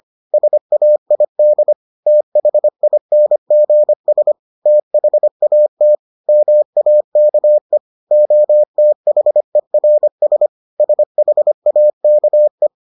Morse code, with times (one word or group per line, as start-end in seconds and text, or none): 0.3-1.7 SAID
2.1-4.3 THINGS
4.7-5.9 THAT
6.3-7.8 MAKE
8.1-10.5 OTHERS
10.8-12.7 SHAKE